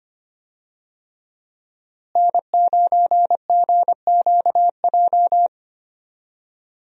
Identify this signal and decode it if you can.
Morse code, heard as a keyed tone, about 25 wpm, 700 Hz. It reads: N9GQJ
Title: USA Call Signs - 5 Characters - Repeated 3x - 25wpm